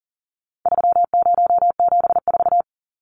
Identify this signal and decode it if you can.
Morse code, heard as a keyed tone, about 40 wpm, 700 Hz. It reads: 3074
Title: Numbers - 4-Digit 40wpm